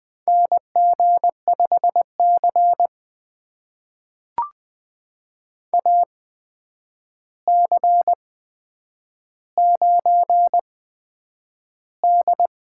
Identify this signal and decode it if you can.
Morse code, heard as a keyed tone, about 20 wpm, spaced as for 9 wpm, 700 Hz. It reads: NG5C AC9D
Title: USA Call Signs - 4 Characters - 8x Character Spacing - 20wpm